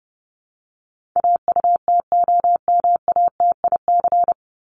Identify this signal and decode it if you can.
Morse code, heard as a keyed tone, about 30 wpm, 700 Hz. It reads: AUTOMATIC
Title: Top 1751-2000 Words 30wpm